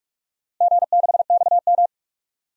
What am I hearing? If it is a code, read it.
Morse code, heard as a keyed tone, about 46 wpm, 700 Hz. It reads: G6XM
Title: DXCC-100 Call Signs - 4 Characters - 45wpm